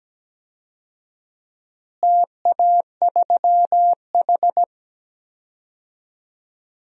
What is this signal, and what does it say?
Morse code, heard as a keyed tone, about 17 wpm, 700 Hz. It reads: TA3H
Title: DXCC-100 Call Signs - 4 Characters - Repeated 2x - 17wpm